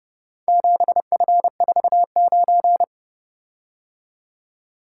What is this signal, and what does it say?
Morse code, heard as a keyed tone, about 31 wpm, 700 Hz. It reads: 7F49